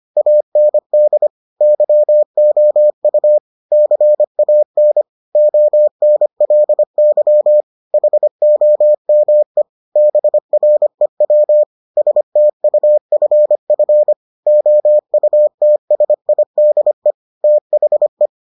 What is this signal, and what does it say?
Morse code, heard as a keyed tone, about 25 wpm, 600 Hz. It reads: AND YOU CAN ONLY HOME BREW STUFF OUTSIDE THE